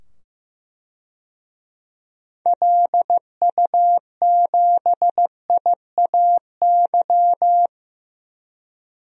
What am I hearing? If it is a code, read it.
Morse code, heard as a keyed tone, about 15 wpm, 700 Hz. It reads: LU7IAY
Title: DXCC-100 Call Signs - 6 Characters - 15wpm